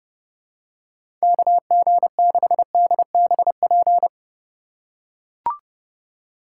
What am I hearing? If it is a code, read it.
Morse code, heard as a keyed tone, about 30 wpm, 700 Hz. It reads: KG6DBP
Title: USA Call Signs - 2x Character Spacing - 30wpm